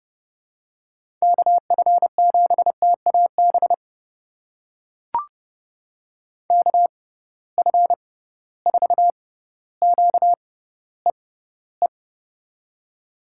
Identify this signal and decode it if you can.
Morse code, heard as a keyed tone, about 30 wpm, spaced as for 15 wpm, 700 Hz. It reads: KF7TAB KF4QEE
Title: USA Call Signs - 6 Characters - 6x Character Spacing - 30wpm